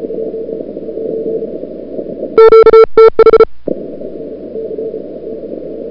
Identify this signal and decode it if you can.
Morse code, heard as a keyed tone, about 34 wpm, 1340 Hz. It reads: QTH
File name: CWSprint / February 2007 CW Sprint / 01-2007 CW Sprint Hour1